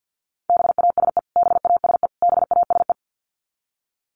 Morse code, heard as a keyed tone, about 50 wpm, 700 Hz. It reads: BASE BASE BASE
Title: Top 500 Words Repeated 50wpm